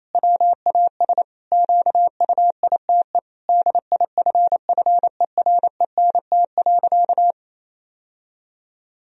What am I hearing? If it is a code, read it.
Morse code, heard as a keyed tone, about 28 wpm, 700 Hz. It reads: WAS QUITE DIFFERENT.
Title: Sentences from Top 1000 Words - Review Part 3 - 28wpm